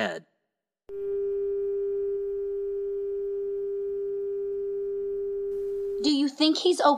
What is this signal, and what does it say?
Dial tone (Japan), 400 Hz, with speech.